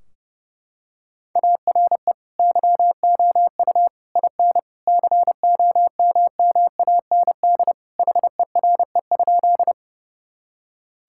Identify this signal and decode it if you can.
Morse code, heard as a keyed tone, about 30 wpm, 700 Hz. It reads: ARE YOU IN COMMAND HERE?
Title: Sentences from Top 1600 Words 30wpm